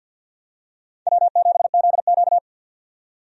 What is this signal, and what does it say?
Morse code, heard as a keyed tone, about 50 wpm, 700 Hz. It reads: W7ZQ
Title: USA Call Signs - 4 Characters - 50wpm